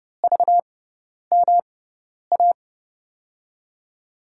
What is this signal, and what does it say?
Morse code, heard as a keyed tone, about 30 wpm, spaced as for 10 wpm, 700 Hz. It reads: VMA
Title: USA Call Signs - 6 Characters - 6x Character Spacing - 30wpm